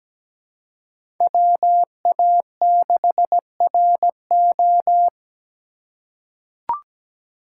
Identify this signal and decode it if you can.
Morse code, heard as a keyed tone, about 17 wpm, 700 Hz. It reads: WA6RO
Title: USA Call Signs - 3x Character Spacing - 17wpm